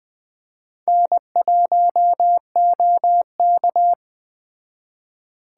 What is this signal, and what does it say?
Morse code, heard as a keyed tone, about 20 wpm, 700 Hz. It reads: N1OK
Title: Speed Racing - USA 1x2 Call Signs 17wpm